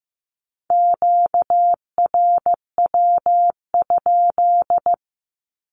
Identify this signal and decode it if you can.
Morse code, heard as a keyed tone, about 15 wpm, 700 Hz. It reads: QRW?
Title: Q-Codes 15wpm